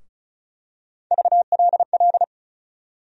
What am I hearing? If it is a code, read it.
Morse code, heard as a keyed tone, about 36 wpm, 700 Hz. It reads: VLL
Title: USA Call Sign Postfixes - 3 Letters 35wpm